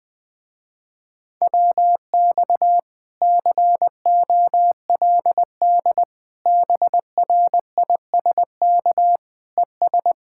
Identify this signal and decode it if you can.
Morse code, heard as a keyed tone, about 20 wpm, 700 Hz. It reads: WX COLD BRISK ES
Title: Speed-Racing Common QSO Phrases Encore - 20wpm